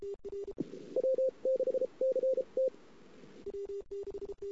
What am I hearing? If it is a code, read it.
Morse code, heard as a keyed tone, about 33 wpm, spaced as for 28 wpm, 520 Hz. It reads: W6CT
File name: CW / 2008-ss-cw-k5tr-op-n5rz -23:00